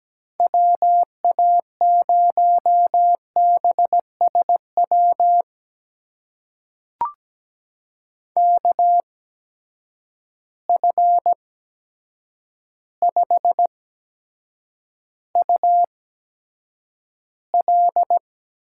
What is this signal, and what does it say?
Morse code, heard as a keyed tone, about 17 wpm, spaced as for 8 wpm, 700 Hz. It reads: WA0BSW KF5UL
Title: USA Call Signs - 6 Characters - 8x Character Spacing - 17wpm